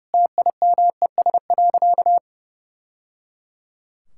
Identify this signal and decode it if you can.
Morse code, heard as a keyed tone, about 29 wpm, 700 Hz. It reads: TIMES.
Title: Sentences from Top 1000 Words - Review Part 6 - 30wpm